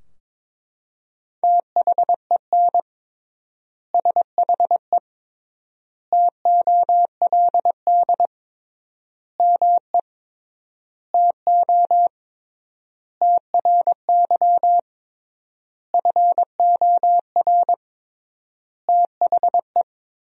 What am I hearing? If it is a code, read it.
Morse code, heard as a keyed tone, about 22 wpm, 700 Hz. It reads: THEN SHE TOLD ME TO TRY FOR THE
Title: 3x Word-Spacing - Sentences from Top 400 Words - 22wpm